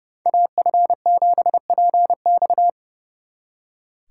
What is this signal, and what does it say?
Morse code, heard as a keyed tone, about 35 wpm, 700 Hz. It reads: AF7PX
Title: Speed Racing - USA 2x2 Call Signs 30wpm